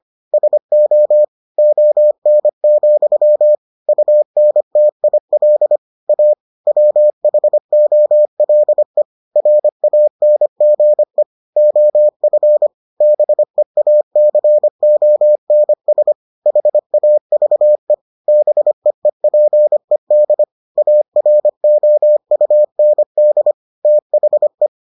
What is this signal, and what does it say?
Morse code, heard as a keyed tone, about 25 wpm, 600 Hz. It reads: SO ON, UNTIL A WHOLE RANGE OF BEACONS HAVE BEEPED AROUND THE